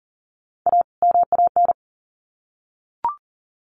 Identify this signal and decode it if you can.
Morse code, heard as a keyed tone, about 40 wpm, 700 Hz. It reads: A MAN
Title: Sets of 2 Words 40wpm